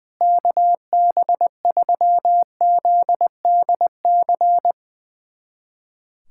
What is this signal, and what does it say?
Morse code, heard as a keyed tone, about 20 wpm, 700 Hz. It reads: KB3ZDC
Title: USA Callsign DE Callsign - 6 Characters - 20wpm